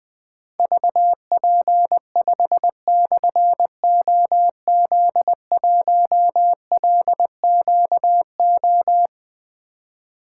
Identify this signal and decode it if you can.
Morse code, heard as a keyed tone, about 20 wpm, 700 Hz. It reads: VP5/OZ1LQO
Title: Difficult Call Signs 20wpm